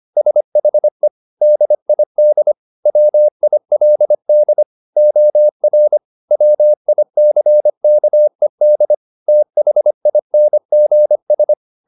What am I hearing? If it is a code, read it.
Morse code, heard as a keyed tone, about 25 wpm, 600 Hz. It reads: SHE DID WILD OR WICKED THINGS
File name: the old man and the sea part 1 25wpm